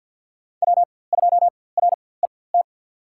Morse code, heard as a keyed tone, about 52 wpm, spaced as for 23 wpm, 700 Hz. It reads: W2RET
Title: USA Call Signs - 5 Characters - 4x Character Spacing - 50wpm